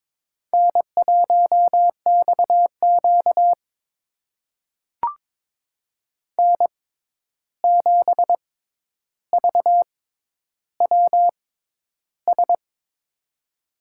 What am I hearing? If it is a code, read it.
Morse code, heard as a keyed tone, about 22 wpm, spaced as for 11 wpm, 700 Hz. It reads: N1XQ N7VWS
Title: USA Call Signs - 6x Character Spacing - 22wpm